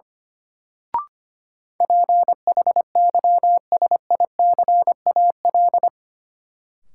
Morse code, heard as a keyed tone, about 25 wpm, 700 Hz. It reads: PHYSICAL
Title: Sentences from Top 1400 Words - New Word Drill Alt - 25wpm